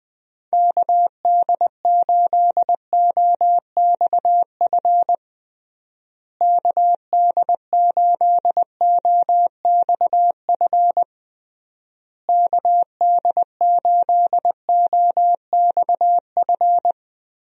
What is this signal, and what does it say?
Morse code, heard as a keyed tone, about 20 wpm, 700 Hz. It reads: KD8OXF KD8OXF KD8OXF
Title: USA Call Signs - 6 Characters - Repeated 3x - 20wpm